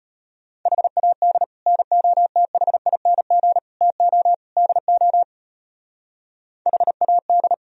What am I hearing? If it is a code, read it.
Morse code, heard as a keyed tone, about 31 wpm, 700 Hz. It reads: HAD NOTHING TO DO HAD
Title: Speed-Racing - Sets of 4 Words 25wpm